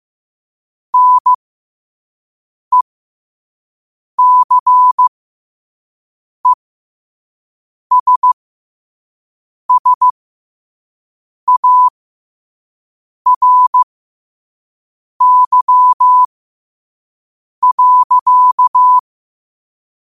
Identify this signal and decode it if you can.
Morse code, heard as a keyed tone, about 15 wpm, spaced as for 5 wpm, 1000 Hz. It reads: NECESSARY.
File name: qotdfiles / 5wpm